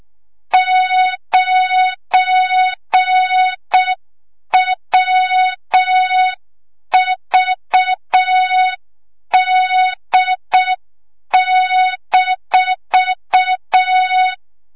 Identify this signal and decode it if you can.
Morse code, heard as a keyed tone, about 6 wpm, 3700 Hz. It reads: 9WVD-
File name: Morze-HA2MN / hang / 25 GYAKORLAT